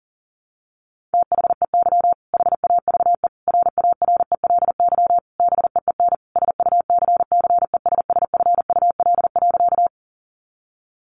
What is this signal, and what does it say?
Morse code, heard as a keyed tone, about 37 wpm, 700 Hz. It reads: THEY HAVE RARELY BEEN SUCCESSFUL.
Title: Sentences from Top 1400 Words 40wpm